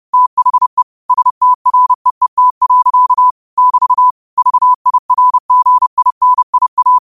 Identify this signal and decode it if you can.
Morse code, heard as a keyed tone, about 30 wpm, 1000 Hz. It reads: THE STREET. <BT> VIRGINIA
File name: qotdfiles / 30wpm